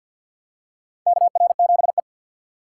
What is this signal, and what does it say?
Morse code, heard as a keyed tone, about 46 wpm, 700 Hz. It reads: KR7E
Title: USA Call Signs - 2x Character Spacing - 50wpm